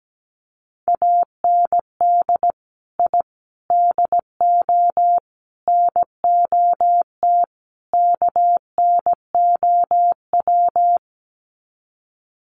Morse code, heard as a keyed tone, about 17 wpm, 700 Hz. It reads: AND I DO NOT KNOW
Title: Sets of 5 Words 17wpm